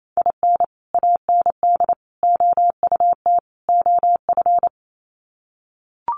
Morse code, heard as a keyed tone, about 28 wpm, 700 Hz. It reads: IN AND OUT OF
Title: Sets of 4 Words 28wpm